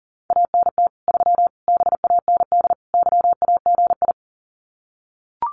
Morse code, heard as a keyed tone, about 40 wpm, 700 Hz. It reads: ANT 3 BAND YAGI
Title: Common QSO Phrases 40wpm